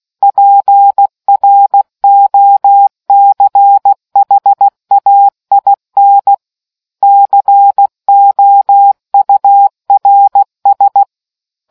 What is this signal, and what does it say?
Morse code, heard as a keyed tone, about 16 wpm, 800 Hz. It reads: PROCHAIN COURS